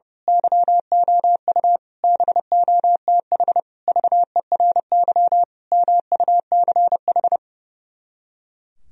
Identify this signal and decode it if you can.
Morse code, heard as a keyed tone, about 28 wpm, 700 Hz. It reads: YOU BOTH VERY MUCH
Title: Speed-Racing - Sets of 5 Words 30wpm